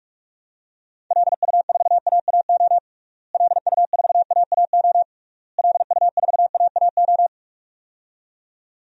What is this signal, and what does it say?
Morse code, heard as a keyed tone, about 44 wpm, 700 Hz. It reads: LU4AAO LU4AAO LU4AAO
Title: DXCC-100 Call Signs - 6 Characters - Repeated 3x - 45wpm